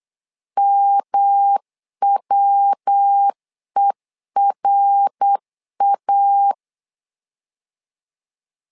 Morse code, heard as a keyed tone, about 8 wpm, 800 Hz. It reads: MWERA
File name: LEZIONI / 3 MWRO / 3 MWR 8wpm 2444